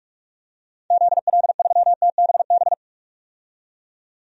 Morse code, heard as a keyed tone, about 46 wpm, 700 Hz. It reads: ZL3TBB
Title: DXCC-100 Call Signs - 6 Characters - 45wpm